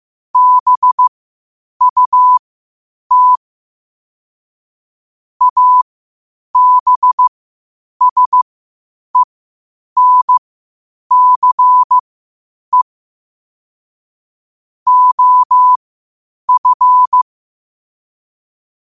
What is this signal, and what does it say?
Morse code, heard as a keyed tone, about 15 wpm, spaced as for 8 wpm, 1000 Hz. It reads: BUT ABSENCE OF
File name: qotdfiles / 8wpm